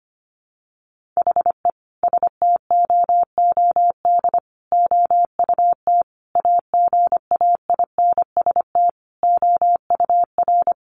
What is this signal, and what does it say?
Morse code, heard as a keyed tone, about 25 wpm, 700 Hz. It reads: HE STOOD OUT AGAINST OUR
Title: Sentences from Top 1500 Words 25wpm